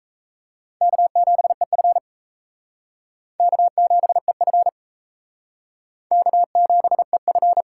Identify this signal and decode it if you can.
Morse code, heard as a keyed tone, about 34 wpm, 700 Hz. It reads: K7EF K7EF K7EF
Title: Speed Racing - USA 1x2 Call Signs 28wpm